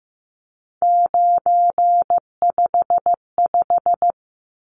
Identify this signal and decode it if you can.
Morse code, heard as a keyed tone, about 15 wpm, 700 Hz. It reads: 955
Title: Numbers - 3-Digit 15wpm